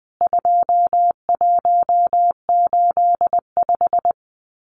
Morse code, heard as a keyed tone, about 20 wpm, 700 Hz. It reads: 2185
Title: CWT Exchange 20wpm